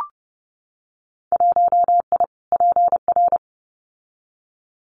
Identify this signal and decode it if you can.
Morse code, heard as a keyed tone, about 30 wpm, 700 Hz. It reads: 1I PR
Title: Winter Field Day Exchanges Encore 30wpm